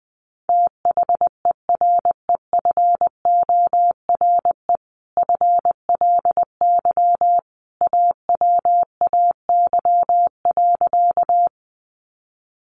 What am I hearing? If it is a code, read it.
Morse code, heard as a keyed tone, about 20 wpm, 700 Hz. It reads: THEREFORE FLY AWAY.
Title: Sentences from Top 800 Words 20wpm